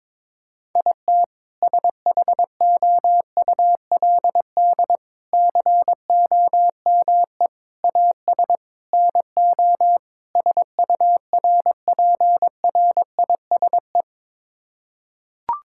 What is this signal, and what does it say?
Morse code, heard as a keyed tone, about 22 wpm, 700 Hz. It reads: IT SHOULD COME AS NO SURPRISE
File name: practice / 22wpm / Sets of 6 Words - 22wpm